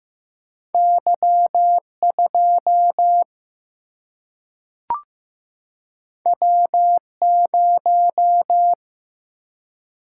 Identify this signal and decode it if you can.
Morse code, heard as a keyed tone, about 15 wpm, 700 Hz. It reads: Y 2 W 0
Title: Single-Letter Single-Number 15wpm